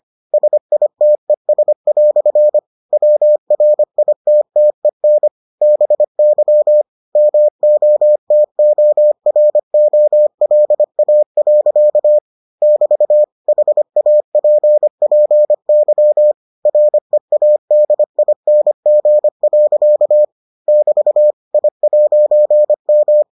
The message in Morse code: SITES" WRITTEN BY MOTOROLA. <BT> HAPPY READING. <BT> I'M

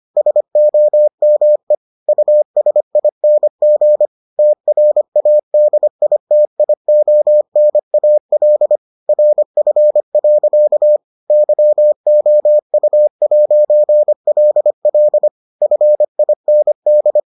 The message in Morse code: SOME USING TRADITIONAL RF. YOU'LL FIND

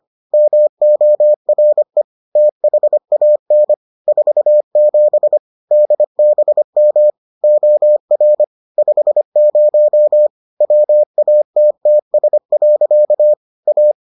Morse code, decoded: MORE THAN 47 DBM OR 50 WATTS. A

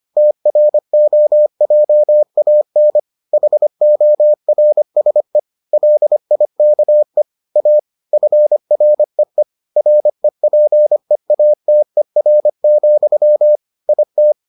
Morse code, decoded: TROJAN HORSE LIKE A FREE REPEATER, IT

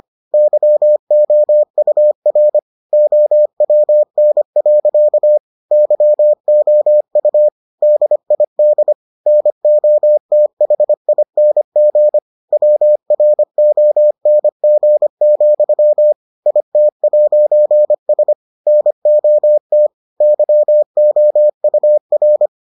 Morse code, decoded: YOUR OWN. YOU DID NOTHING WRONG, IT'S NOT YOUR

